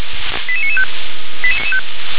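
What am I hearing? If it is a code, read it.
Five-tone selective call (ZVEI), heard as quick runs of five tones, tones of about 70 ms. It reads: 88995
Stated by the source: ZVEI five-tone call 88992 received on an SDR.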